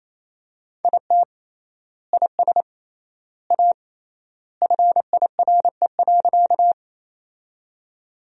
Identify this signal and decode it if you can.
Morse code, heard as a keyed tone, about 28 wpm, 700 Hz. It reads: IT IS A FIRE.
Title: 3x Word-Spacing - Sentences from Top 400 Words - 28wpm